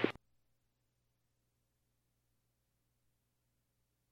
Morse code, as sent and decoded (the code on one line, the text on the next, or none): none